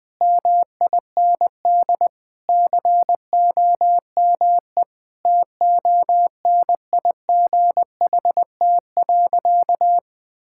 Morse code, heard as a keyed tone, about 20 wpm, 700 Hz. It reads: MIND COME TONIGHT.
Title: Sentences with 4-Word Phrases - 20wpm